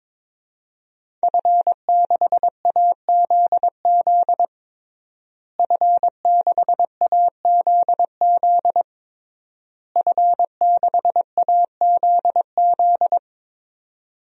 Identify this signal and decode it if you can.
Morse code, heard as a keyed tone, about 22 wpm, 700 Hz. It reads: F6AZZ F6AZZ F6AZZ
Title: DXCC-100 Call Signs - 5 Characters - Repeated 3x - 22wpm